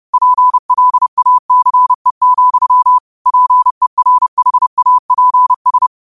Morse code, heard as a keyed tone, about 30 wpm, 1000 Hz. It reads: PLACE, PERHAPS